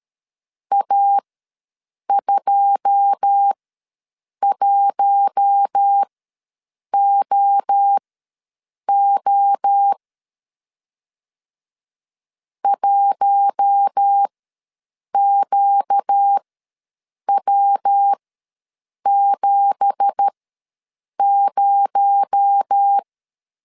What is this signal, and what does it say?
Morse code, heard as a keyed tone, about 12 wpm, spaced as for 7 wpm, 800 Hz. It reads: A21OO 1QW70
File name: LEZIONI / 9 2867 ws 12wpm 3431